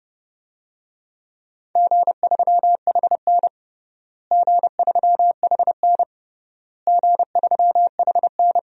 Morse code, heard as a keyed tone, about 30 wpm, 700 Hz. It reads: G3HN G3HN G3HN
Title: DXCC-100 Call Signs - 4 Characters - Repeated 3x - 30wpm